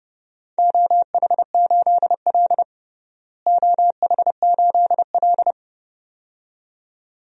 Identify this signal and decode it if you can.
Morse code, heard as a keyed tone, about 30 wpm, 700 Hz. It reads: OH8L OH8L
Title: DXCC-100 Call Signs - 4 Characters - Repeated 2x - 30wpm